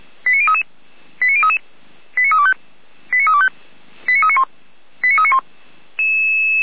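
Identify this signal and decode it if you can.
Five-tone selective call (ZVEI), heard as quick runs of five tones, tones of about 80 ms. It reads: 89033 89033 89325 89325 89391 89391
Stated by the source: ZVEI five-tone call 89512 received by SDR.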